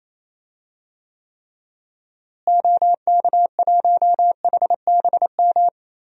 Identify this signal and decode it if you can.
Morse code, heard as a keyed tone, about 28 wpm, 700 Hz. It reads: OK1HBM
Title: DXCC-100 Call Signs - 6 Characters - Repeated 2x - 28wpm